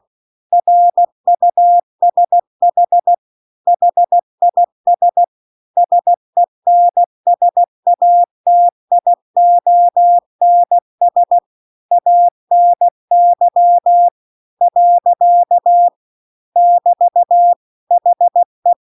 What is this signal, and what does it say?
Morse code, heard as a keyed tone, about 16 wpm, 700 Hz. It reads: RUSH HIS SENSATIONS ANY . <BT> HE
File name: CW-The-Big-Twoheaded-River-2-16WPM